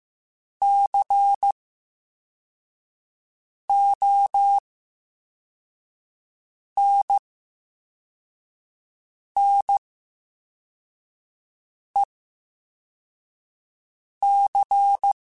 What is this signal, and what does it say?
Morse code, heard as a keyed tone, about 15 wpm, 800 Hz. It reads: C O N N E C
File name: Special INGs-mp3s / Special INGs-mp3s / VerySlowToFaster Ing @13